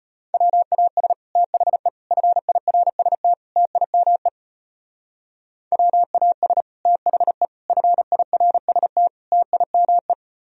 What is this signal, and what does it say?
Morse code, heard as a keyed tone, about 31 wpm, 700 Hz. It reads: WAS THE FIRST TIME WAS THE FIRST TIME
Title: Speed-Racing - Sets of 4 Words 25wpm